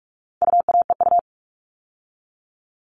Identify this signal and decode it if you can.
Morse code, heard as a keyed tone, about 44 wpm, 700 Hz. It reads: UAEU
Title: All Cut Numbers - 4-Digit 45wpm